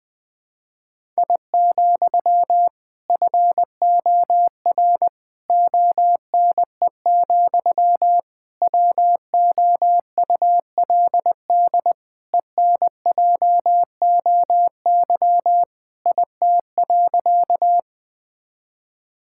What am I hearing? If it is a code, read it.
Morse code, heard as a keyed tone, about 21 wpm, 700 Hz. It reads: I, FOR ONE, WOULD ENJOY IT.
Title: Sentences from Top 1000 Words - Review Part 5 - 20wpm